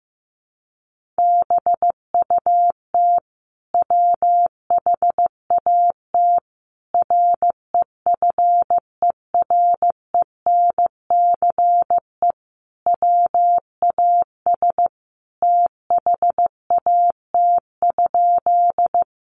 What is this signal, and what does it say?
Morse code, heard as a keyed tone, about 15 wpm, 700 Hz. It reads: BUT WHAT REFERENCE WAS THAT?
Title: Sentences from Top 1200 Words 15wpm